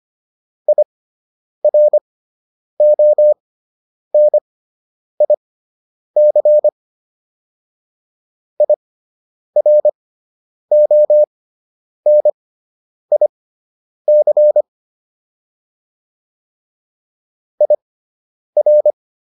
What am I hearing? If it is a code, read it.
Morse code, heard as a keyed tone, about 25 wpm, spaced as for 9 wpm, 600 Hz. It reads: IRONIC IRONIC IR